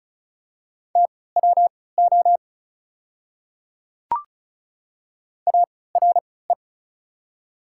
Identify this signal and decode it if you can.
Morse code, heard as a keyed tone, about 36 wpm, spaced as for 20 wpm, 700 Hz. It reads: TWO ARE
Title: 3x Character Spacing - Top 100 Words - 35wpm